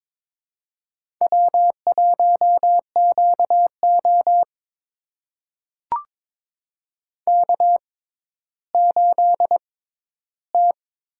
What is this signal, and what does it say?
Morse code, heard as a keyed tone, about 22 wpm, spaced as for 13 wpm, 700 Hz. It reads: W1QO K8T
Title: USA Call Signs - 4 Characters - 6x Character Spacing - 22wpm